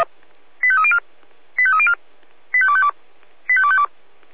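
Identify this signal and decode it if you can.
Five-tone selective call (ZVEI), heard as quick runs of five tones, tones of about 70 ms. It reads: 85304 85304 85252 85252
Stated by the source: ZVEI five-tone selective call 85352 received by SDR.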